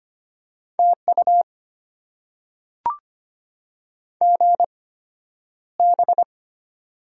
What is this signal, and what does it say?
Morse code, heard as a keyed tone, about 25 wpm, spaced as for 11 wpm, 690 Hz. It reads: TU GB
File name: Top 90 QSO Elements - 8x Character Spacing - 25wpm